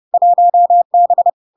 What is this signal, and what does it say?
Morse code, heard as a keyed tone, about 30 wpm, 700 Hz. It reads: 1B
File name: LARK-FD-2022-Log-in-CW-30wpm-700-Hz